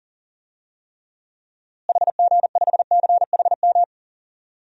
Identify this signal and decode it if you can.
Morse code, heard as a keyed tone, about 40 wpm, 700 Hz. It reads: HG5CHM
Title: DXCC-100 Call Signs - 6 Characters - Repeated 3x - 40wpm